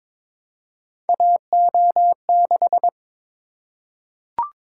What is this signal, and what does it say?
Morse code, heard as a keyed tone, about 22 wpm, 700 Hz. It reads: AO6
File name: practice / 22wpm / Three Letter-Number Combinations - 22wpm